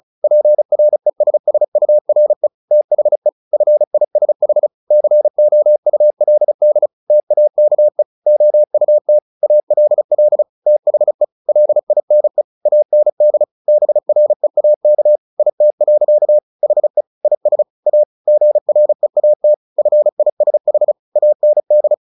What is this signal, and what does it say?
Morse code, heard as a keyed tone, about 34 wpm, 600 Hz. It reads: PRESSURE THE FISH COULD TAKE OUT ALL THE LINE AND BREAK IT. HE IS A GREAT FISH AND
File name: the old man and the sea / the old man and the sea 1 35wpm